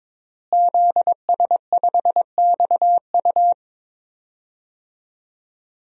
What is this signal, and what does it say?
Morse code, heard as a keyed tone, about 22 wpm, 700 Hz. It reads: ZS5XU
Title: DXCC-100 Call Signs - 5 Characters - Repeated 3x - 22wpm